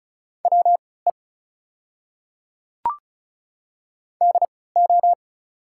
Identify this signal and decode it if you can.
Morse code, heard as a keyed tone, about 36 wpm, spaced as for 20 wpm, 690 Hz. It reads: WE DO
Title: 3x Character Spacing - Top 300 Words - 35wpm